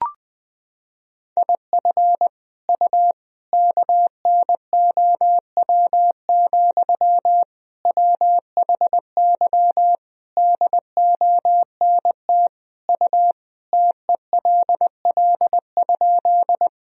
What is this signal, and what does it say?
Morse code, heard as a keyed tone, about 20 wpm, 700 Hz. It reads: IF U KNOW, WHY DONT U TELL?
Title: Sentences from Top 300 Words and Abbreviations 20wpm